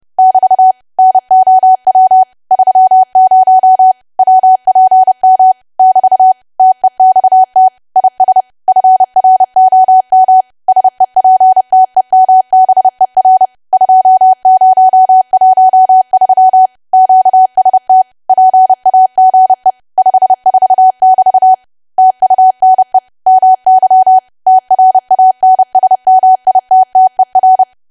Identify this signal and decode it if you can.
Morse code, heard as a keyed tone, about 30 wpm, 750 Hz. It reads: <BT> NOW 30 WPM <BT> TEXT IS FROM SEPTEMBER 2013 QST PAGE 54<BT> TUNE MY TRANSMITTER
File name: cwa / W1AW-Level3 / 30WPM